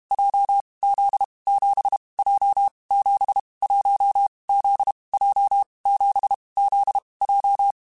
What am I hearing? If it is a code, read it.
Morse code, heard as a keyed tone, about 32 wpm, 800 Hz. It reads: J Z 7 J 7 1 Z J 7 Z J
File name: Basic Letter Groups By7Groups / JZ17 / JZ17@30